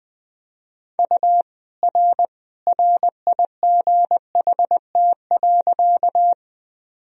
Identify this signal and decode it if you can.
Morse code, heard as a keyed tone, about 20 wpm, 700 Hz. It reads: U R RIGHT.
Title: Sentences from Top 200 Words and Abbreviations 20wpm